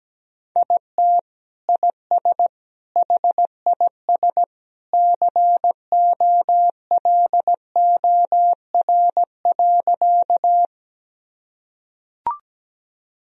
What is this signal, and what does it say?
Morse code, heard as a keyed tone, about 17 wpm, 700 Hz. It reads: IT IS HIS COLOR.
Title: Sentences from Top 1000 Words - Review Part 2 - 17wpm